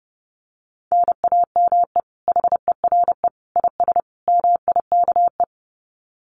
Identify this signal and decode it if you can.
Morse code, heard as a keyed tone, about 30 wpm, 700 Hz. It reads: NAME HERE IS MIKE
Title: Common QSO Phrases 30wpm